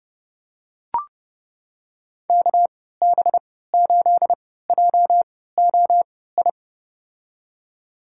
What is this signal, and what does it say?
Morse code, heard as a keyed tone, about 32 wpm, spaced as for 17 wpm, 700 Hz. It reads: KB8JOI
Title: USA Call Signs - 6 Characters - 3x Character Spacing - 30wpm